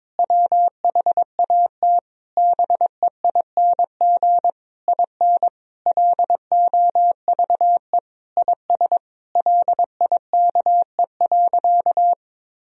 Morse code, heard as a keyed tone, about 22 wpm, 700 Hz. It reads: WHAT BEING IN LOVE IS LIKE.
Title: Sentences from Top 1000 Words - Review Part 3 - 22wpm